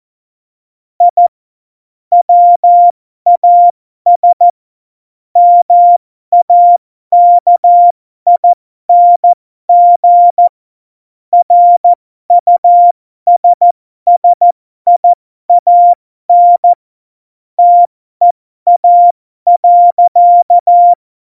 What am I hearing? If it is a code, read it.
Morse code, heard as a keyed tone, about 14 wpm, spaced as for 12 wpm, 700 Hz. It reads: I WAS MAKING RUSSIAN TEA.
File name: Morsekurs / Witze